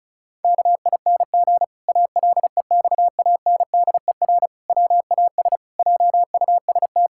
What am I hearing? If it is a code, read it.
Morse code, heard as a keyed tone, about 36 wpm, 700 Hz. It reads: KING ALEXANDER WAS JUST
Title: Pangram Challenge v2 - 35wpm